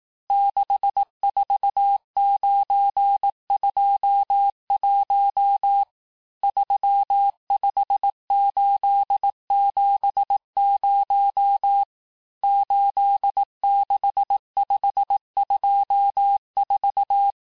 Morse code, heard as a keyed tone, about 18 wpm, 800 Hz. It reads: 64921 35870 86524
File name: cours-du-lundi files / cours-1594--teste-chiffre-vitesse-17mpm